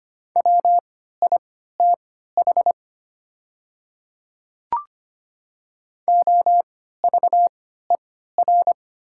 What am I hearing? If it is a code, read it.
Morse code, heard as a keyed tone, about 25 wpm, spaced as for 14 wpm, 700 Hz. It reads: WITH OVER
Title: 3x Character Spacing - Top 100 Words - 25wpm